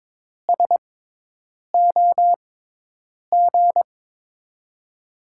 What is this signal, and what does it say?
Morse code, heard as a keyed tone, about 21 wpm, spaced as for 8 wpm, 700 Hz. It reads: SOG
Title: USA Call Signs - 6 Characters - 6x Character Spacing - 22wpm